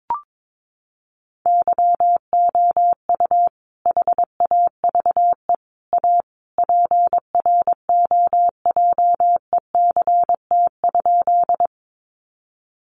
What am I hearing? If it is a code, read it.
Morse code, heard as a keyed tone, about 22 wpm, 700 Hz. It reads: YOU HAVE A PROJECT?
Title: Sentences from Top 1400 Words 22wpm